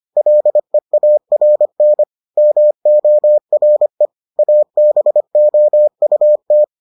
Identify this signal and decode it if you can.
Morse code, heard as a keyed tone, about 25 wpm, 600 Hz. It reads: LEARN MORE ABOUT